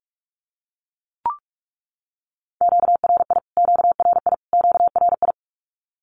Morse code, heard as a keyed tone, about 44 wpm, 700 Hz. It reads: QRI QRI QRI
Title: Q-Codes Repeated 3x 45wpm